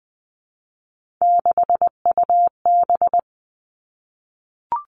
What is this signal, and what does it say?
Morse code, heard as a keyed tone, about 20 wpm, 700 Hz. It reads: 6UB
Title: Three Challenging Letter-Number Combinations 20wpm